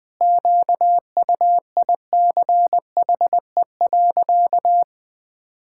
Morse code, heard as a keyed tone, about 21 wpm, 700 Hz. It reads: QUICHE.
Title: Pangram Challenge v2 - 20wpm